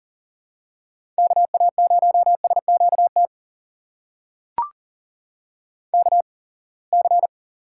Morse code, heard as a keyed tone, about 38 wpm, spaced as for 27 wpm, 700 Hz. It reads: KA0SQT KC
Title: USA Call Signs - 6 Characters - 8x Character Spacing - 40wpm